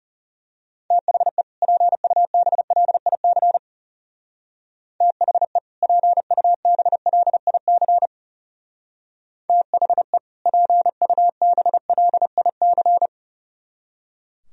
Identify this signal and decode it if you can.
Morse code, heard as a keyed tone, about 34 wpm, 700 Hz. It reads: THE PUBLIC THE PUBLIC THE PUBLIC
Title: Speed-Racing - Sets of 2 Words 30wpm